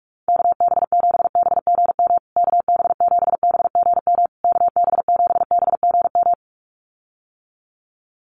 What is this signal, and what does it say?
Morse code, heard as a keyed tone, about 44 wpm, 690 Hz. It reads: KB7BGM KB7BGM KB7BGM
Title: USA Call Signs Repeated 3x 45wpm